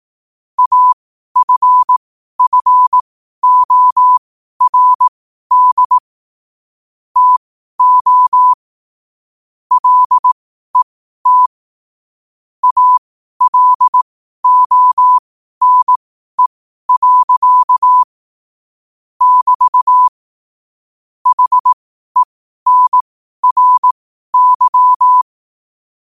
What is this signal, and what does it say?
Morse code, heard as a keyed tone, about 18 wpm, spaced as for 12 wpm, 1000 Hz. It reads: AFFORD TO LET ALONE. <BT> HENRY
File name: qotdfiles / 12wpm